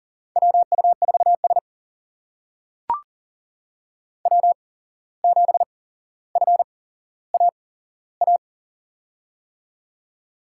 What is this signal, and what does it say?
Morse code, heard as a keyed tone, about 40 wpm, spaced as for 16 wpm, 700 Hz. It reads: WU4S W7FAA
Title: USA Call Signs - 8x Character Spacing - 40wpm